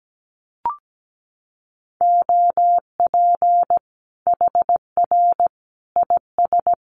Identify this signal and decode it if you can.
Morse code, heard as a keyed tone, about 17 wpm, 700 Hz. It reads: OP HR IS
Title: Common QSO Phrases 17wpm